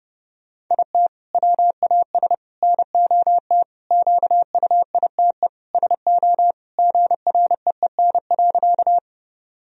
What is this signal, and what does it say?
Morse code, heard as a keyed tone, about 30 wpm, 700 Hz. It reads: IT WAS NOT QUITE SO GREEN.
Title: Sentences from Top 1000 Words - Review Part 2 - 30wpm